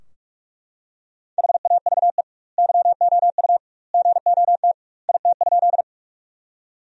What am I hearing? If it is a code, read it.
Morse code, heard as a keyed tone, about 42 wpm, 700 Hz. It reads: HAVE YOU GOT IT?
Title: 2x Word-Spacing - Sentences from Top 300 Words - 45wpm